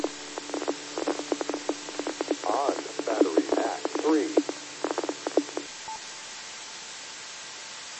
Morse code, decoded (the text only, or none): none